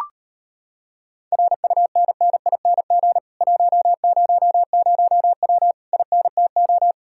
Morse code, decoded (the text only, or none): RUNNING 100W INTO